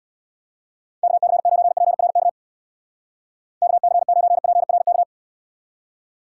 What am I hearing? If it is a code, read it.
Morse code, heard as a keyed tone, about 79 wpm, 700 Hz. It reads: KC9PRK KC9PRK